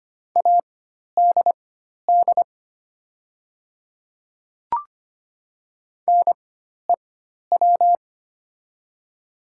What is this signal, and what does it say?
Morse code, heard as a keyed tone, about 25 wpm, spaced as for 12 wpm, 690 Hz. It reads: ADD NEW
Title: 4x Character Spacing - Top 300 Words - 25wpm